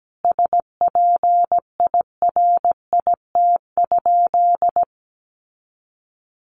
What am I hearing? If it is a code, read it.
Morse code, heard as a keyed tone, about 17 wpm, 700 Hz. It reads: SPIRIT?
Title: Sentences from Top 1400 Words 17wpm